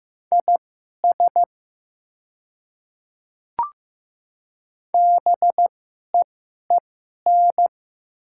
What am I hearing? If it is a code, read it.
Morse code, heard as a keyed tone, about 15 wpm, spaced as for 11 wpm, 700 Hz. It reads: IS BEEN
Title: 2x Character Spacing - Top 200 Words - 15wpm